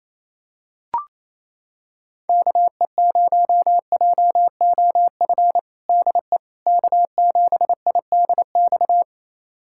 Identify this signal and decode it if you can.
Morse code, heard as a keyed tone, about 28 wpm, 700 Hz. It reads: KE0JOF DE K7IDX
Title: USA Callsign DE Callsign 28wpm